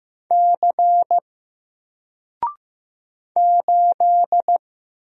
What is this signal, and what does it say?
Morse code, heard as a keyed tone, about 15 wpm, 700 Hz. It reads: C 8
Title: Single Letter-Number 15wpm